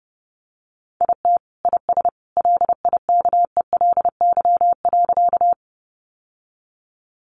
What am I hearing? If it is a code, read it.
Morse code, heard as a keyed tone, about 30 wpm, 700 Hz. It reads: IT IS LIKELY.
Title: Sentences from Top 600 Words 30wpm